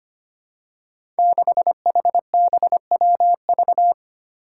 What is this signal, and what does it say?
Morse code, heard as a keyed tone, about 25 wpm, 700 Hz. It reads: 6HBWV
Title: Five Challenging Letter-Number Combinations 25wpm